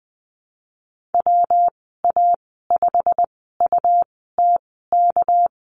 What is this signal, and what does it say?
Morse code, heard as a keyed tone, about 20 wpm, spaced as for 15 wpm, 700 Hz. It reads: WA5UTK
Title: 219 Lesson 35g - Number 0 - Call Signs 20wpm